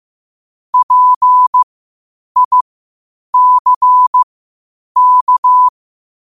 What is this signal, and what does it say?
Morse code, heard as a keyed tone, about 15 wpm, spaced as for 8 wpm, 1000 Hz. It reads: PICK
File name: qotdfiles / 8wpm